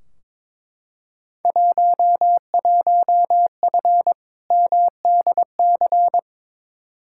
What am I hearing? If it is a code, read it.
Morse code, heard as a keyed tone, about 22 wpm, 700 Hz. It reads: 11F MDC
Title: Field Day Exchanges Encore - 22wpm